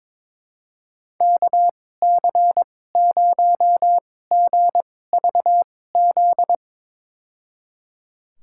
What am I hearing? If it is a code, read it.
Morse code, heard as a keyed tone, about 22 wpm, spaced as for 16 wpm, 700 Hz. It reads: KC0GVZ
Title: USA Call Signs - 2x Character Spacing - 22wpm